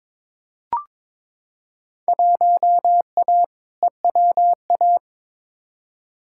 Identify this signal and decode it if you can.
Morse code, heard as a keyed tone, about 22 wpm, 700 Hz. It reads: 1A EWA
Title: Field Day Exchanges Encore - 22wpm